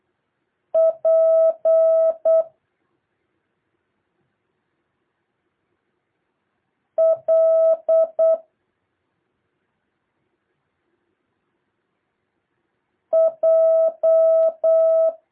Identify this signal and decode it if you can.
Morse code, heard as a keyed tone, about 8 wpm, 650 Hz. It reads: P L J
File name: jh8cbh / 2023cw